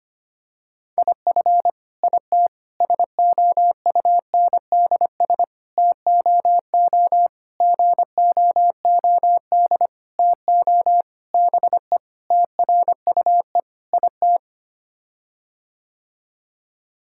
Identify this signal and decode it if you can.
Morse code, heard as a keyed tone, about 25 wpm, 700 Hz. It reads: IF IT SOUNDS TOO GOOD TO BE TRUE IT
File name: Sets of 9 Words - 25wpm